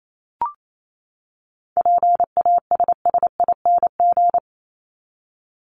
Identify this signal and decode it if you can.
Morse code, heard as a keyed tone, about 28 wpm, 700 Hz. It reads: PASSING
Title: ING Suffix - Top 100 Words with Base Words 28wpm